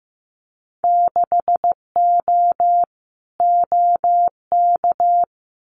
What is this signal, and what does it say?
Morse code, heard as a keyed tone, about 15 wpm, 700 Hz. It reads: 6O OK
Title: Winter Field Day Exchanges Encore 15wpm